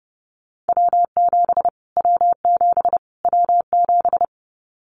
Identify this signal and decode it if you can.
Morse code, heard as a keyed tone, about 30 wpm, 700 Hz. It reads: W7 W7 W7
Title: Callsign Prefixes for Top 25 DXCC - Repeated 3x - 30wpm